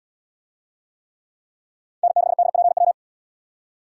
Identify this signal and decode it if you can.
Morse code, heard as a keyed tone, about 73 wpm, 700 Hz. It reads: N6DPW